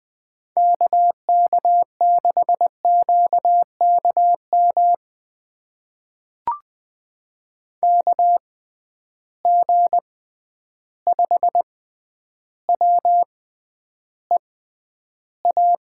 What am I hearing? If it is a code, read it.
Morse code, heard as a keyed tone, about 20 wpm, spaced as for 10 wpm, 700 Hz. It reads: KK6QKM KG5WEA
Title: USA Call Signs - 6 Characters - 6x Character Spacing - 20wpm